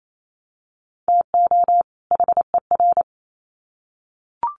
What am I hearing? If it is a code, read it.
Morse code, heard as a keyed tone, about 28 wpm, 700 Hz. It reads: TO HER
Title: Sets of 2 Words 28wpm